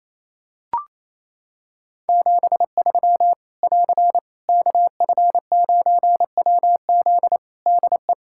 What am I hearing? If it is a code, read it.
Morse code, heard as a keyed tone, about 27 wpm, 700 Hz. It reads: 73 <AR> KF9WZ DE
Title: Practice QSO - Exchange 4 - Ending - 28wpm